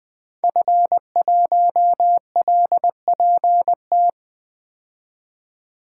Morse code, heard as a keyed tone, about 21 wpm, 700 Hz. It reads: F1LPT
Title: 2023 POTA Activator Call Signs - 20wpm